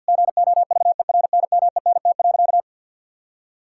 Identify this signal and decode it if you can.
Morse code, heard as a keyed tone, about 50 wpm, 700 Hz. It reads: GOVERNMENT.
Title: Sentences from Top 1000 Words - Review Part 5 - 50wpm